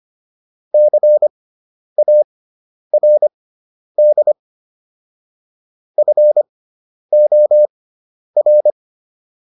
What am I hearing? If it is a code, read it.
Morse code, heard as a keyed tone, about 25 wpm, spaced as for 10 wpm, 600 Hz. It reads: CARD FOR